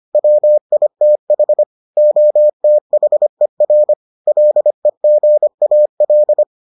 Morse code, heard as a keyed tone, about 25 wpm, 600 Hz. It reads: WITH OTHER LEGAL